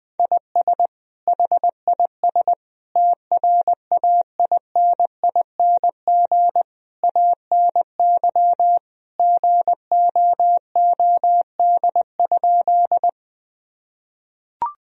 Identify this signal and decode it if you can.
Morse code, heard as a keyed tone, about 21 wpm, 700 Hz. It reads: IS HIS TRAINING ANY GOOD?
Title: Sentences from Top 1000 Words - Review Part 4 - 20wpm